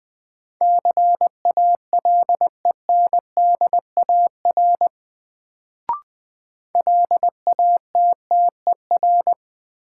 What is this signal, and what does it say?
Morse code, heard as a keyed tone, about 20 wpm, 700 Hz. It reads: CALENDAR LATTER
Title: Sentences from Top 1600 Words - New Word Drill - 20wpm